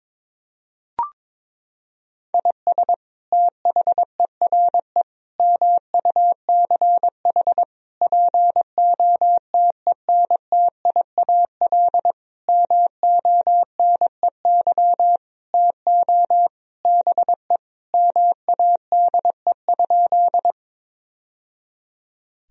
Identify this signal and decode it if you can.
Morse code, heard as a keyed tone, about 22 wpm, 700 Hz. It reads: IS THERE MUCH POTENTIAL MONEY TO BE MADE?
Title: Sentences from Top 1000 Words - Review Part 4 - 22wpm